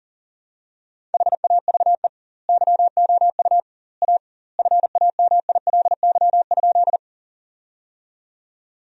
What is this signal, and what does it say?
Morse code, heard as a keyed tone, about 40 wpm, 700 Hz. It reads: HAVE YOU A FAMILY?
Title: 2x Word-Spacing - Sentences from Top 400 Words - 40wpm